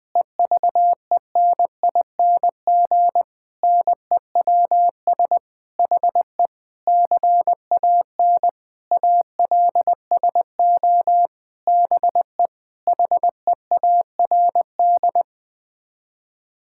Morse code, heard as a keyed tone, about 20 wpm, 700 Hz. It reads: EVENING NEWS HE CAN ALSO BE HEARD
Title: Sets of 9 Words - 20wpm